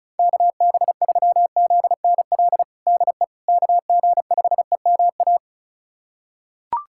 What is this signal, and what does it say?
Morse code, heard as a keyed tone, about 36 wpm, 700 Hz. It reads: KB3ZNL DE KG5EMA
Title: USA Callsign DE Callsign - 6 Characters - 35wpm